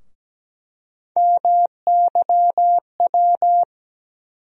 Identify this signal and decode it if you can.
Morse code, heard as a keyed tone, about 17 wpm, 700 Hz. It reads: MYW
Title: USA Call Sign Postfixes - 3 Letters 17wpm